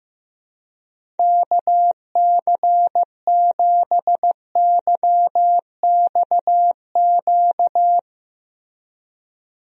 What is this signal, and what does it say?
Morse code, heard as a keyed tone, about 16 wpm, 700 Hz. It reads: KC7YXQ
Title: USA Call Signs - 6 Characters - Repeated 3x - 15wpm